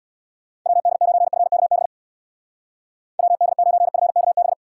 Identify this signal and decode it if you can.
Morse code, heard as a keyed tone, about 66 wpm, 700 Hz. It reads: WD9FKB WD9FKB